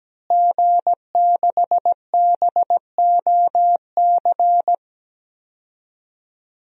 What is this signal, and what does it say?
Morse code, heard as a keyed tone, about 17 wpm, 700 Hz. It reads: G6BOC
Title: DXCC-100 Call Signs - 5 Characters - Repeated 3x - 17wpm